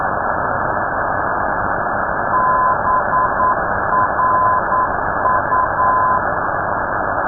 Morse code, heard as a keyed tone, about 9 wpm, 1020 Hz. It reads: BRU
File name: europe / d-3079-cw-id-2005-rb